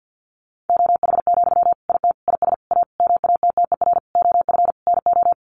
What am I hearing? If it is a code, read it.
Morse code, heard as a keyed tone, about 50 wpm, 700 Hz. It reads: OH, IT IS A MATTER OF NO
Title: Sentences from Top 1200 Words 50wpm